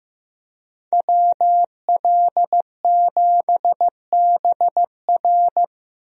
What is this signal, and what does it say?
Morse code, heard as a keyed tone, about 15 wpm, 700 Hz. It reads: WL7BR
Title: USA Call Signs - 5 Characters - Repeated 3x - 15wpm